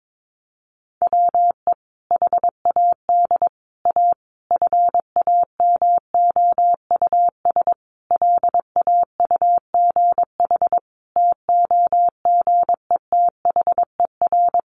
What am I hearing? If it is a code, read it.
Morse code, heard as a keyed tone, about 22 wpm, 700 Hz. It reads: WE HAD A FAMOUS LAUGH TOGETHER
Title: Sentences from Top 1500 Words 22wpm